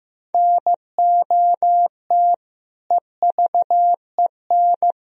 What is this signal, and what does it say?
Morse code, heard as a keyed tone, about 16 wpm, 700 Hz. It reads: NOT EVEN
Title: Sentences from Top 1000 Words - Review Part 1 - 15wpm